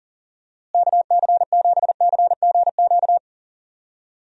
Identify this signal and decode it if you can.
Morse code, heard as a keyed tone, about 40 wpm, 700 Hz. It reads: KC7CGQ
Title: USA Call Signs - 6 Characters - Repeated 2x - 40wpm